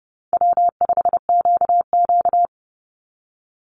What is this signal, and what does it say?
Morse code, heard as a keyed tone, about 30 wpm, 700 Hz. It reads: W5QQ
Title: Call Signs Repeated 3x - 30wpm